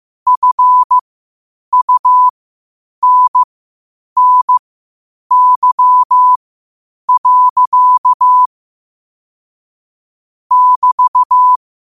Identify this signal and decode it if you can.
Morse code, heard as a keyed tone, about 15 wpm, spaced as for 8 wpm, 1000 Hz. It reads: FUNNY. <BT>
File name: qotdfiles / 8wpm